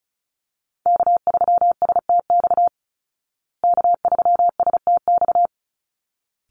Morse code, heard as a keyed tone, about 34 wpm, 700 Hz. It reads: K3STX K3STX
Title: Call Signs Repeated 3x - 35wpm